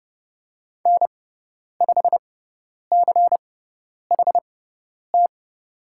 Morse code, heard as a keyed tone, about 30 wpm, spaced as for 10 wpm, 700 Hz. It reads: N5CHT
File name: Lesson 17g - Letter u - Call Signs 30wpm